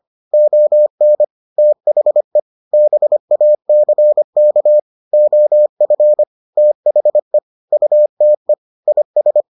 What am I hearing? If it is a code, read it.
Morse code, heard as a keyed tone, about 25 wpm, 600 Hz. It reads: ON THE BACK OF THE UTE IS